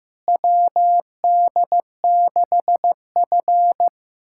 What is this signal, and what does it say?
Morse code, heard as a keyed tone, about 15 wpm, 700 Hz. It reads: WD6F
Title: USA Callsign DE Callsign - 4 Characters - 15wpm